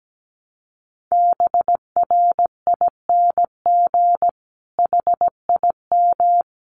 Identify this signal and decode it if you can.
Morse code, heard as a keyed tone, about 17 wpm, 700 Hz. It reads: BRING HIM
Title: Sentences from Top 1300 Words 17wpm